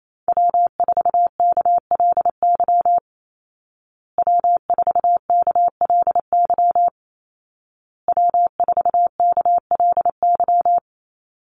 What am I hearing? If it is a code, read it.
Morse code, heard as a keyed tone, about 28 wpm, 700 Hz. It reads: W4KLY W4KLY W4KLY
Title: Call Signs Repeated 2x - 28wpm